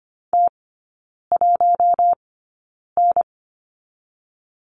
Morse code, heard as a keyed tone, about 25 wpm, spaced as for 9 wpm, 700 Hz. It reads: T1N
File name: course / 25:10wpm / 093 Lesson 16e - Letter c - Three Characters 25wpm